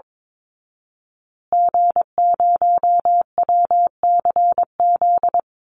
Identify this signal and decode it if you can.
Morse code, heard as a keyed tone, about 22 wpm, 700 Hz. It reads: G0WCZ
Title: Call Signs Repeated 3x - 22wpm